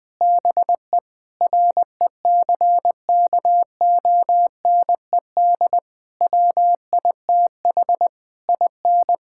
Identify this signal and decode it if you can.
Morse code, heard as a keyed tone, about 20 wpm, 700 Hz. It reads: BE RECKONED WITH IN
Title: Sentences with 6-Word Phrases - 20wpm